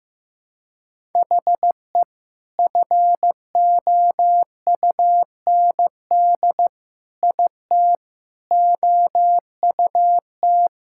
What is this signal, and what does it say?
Morse code, heard as a keyed tone, about 15 wpm, 700 Hz. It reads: HE FOUND IT OUT
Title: Sentences from Top 1600 Words 15wpm